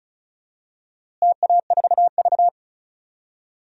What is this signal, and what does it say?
Morse code, heard as a keyed tone, about 36 wpm, 700 Hz. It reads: TA4V
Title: DXCC-100 Call Signs - 4 Characters - Repeated 3x - 35wpm